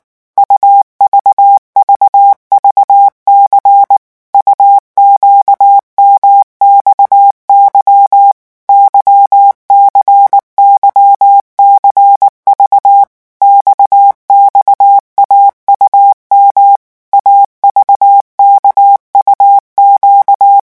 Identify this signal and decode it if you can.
Morse code, heard as a keyed tone, about 19 wpm, 800 Hz. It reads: UVVVC UQMXY YCYCV XXAUM AVKUQ
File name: morse mp3 / revision2-vitesse 18 mots